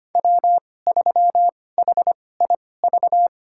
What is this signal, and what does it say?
Morse code, heard as a keyed tone, about 25 wpm, spaced as for 18 wpm, 700 Hz. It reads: W3HIV
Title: USA Call Signs - 5 Characters - 2x Character Spacing - 25wpm